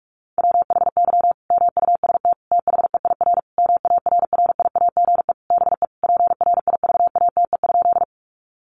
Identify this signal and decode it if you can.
Morse code, heard as a keyed tone, about 44 wpm, 700 Hz. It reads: WHY MUST THEIR MARRIAGE BE PRIVATE?
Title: Sentences from Top 1000 Words 45wpm